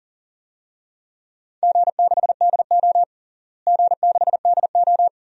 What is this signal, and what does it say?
Morse code, heard as a keyed tone, about 40 wpm, 700 Hz. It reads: G6DO G6DO
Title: DXCC-100 Call Signs - 4 Characters - Repeated 2x - 40wpm